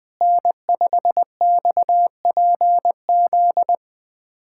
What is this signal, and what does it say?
Morse code, heard as a keyed tone, about 20 wpm, 700 Hz. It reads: N5XPZ
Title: USA Callsign DE Callsign - 5 Characters - 20wpm